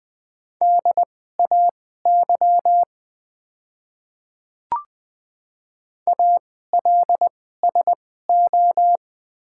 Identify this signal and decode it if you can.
Morse code, heard as a keyed tone, about 20 wpm, spaced as for 15 wpm, 700 Hz. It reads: DAY ALSO
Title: 2x Character Spacing - Top 200 Words - 20wpm